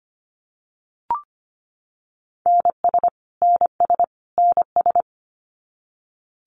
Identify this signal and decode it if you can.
Morse code, heard as a keyed tone, about 25 wpm, 700 Hz. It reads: NS NS NS
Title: Canadian Provinces and Territories - Repeated 3x - 25wpm